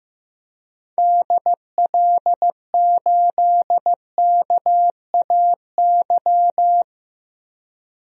Morse code, heard as a keyed tone, about 15 wpm, 700 Hz. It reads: DL8KAY
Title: DXCC-100 Call Signs - 6 Characters - 15wpm